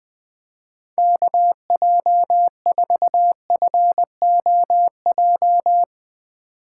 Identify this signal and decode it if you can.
Morse code, heard as a keyed tone, about 20 wpm, 700 Hz. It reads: KJ4FOJ